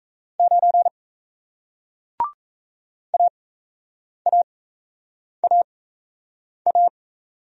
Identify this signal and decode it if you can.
Morse code, heard as a keyed tone, about 38 wpm, spaced as for 8 wpm, 700 Hz. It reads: 9 AAAA